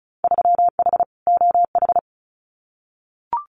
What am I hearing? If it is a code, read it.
Morse code, heard as a keyed tone, about 34 wpm, 700 Hz. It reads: 3H OH
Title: Winter Field Day Exchanges Encore 35wpm